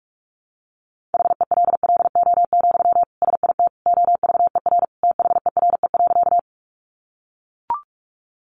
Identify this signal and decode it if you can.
Morse code, heard as a keyed tone, about 44 wpm, 700 Hz. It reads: HELLO, SIT OVER THERE.
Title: Sentences from Top 1100 Words 45wpm